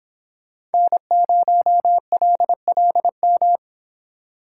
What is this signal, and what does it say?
Morse code, heard as a keyed tone, about 31 wpm, 700 Hz. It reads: N0LLM